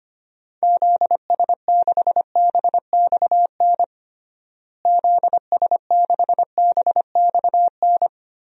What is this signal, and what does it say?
Morse code, heard as a keyed tone, about 25 wpm, 700 Hz. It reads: ZS6BXN ZS6BXN
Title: DXCC-100 Call Signs - 6 Characters - Repeated 3x - 25wpm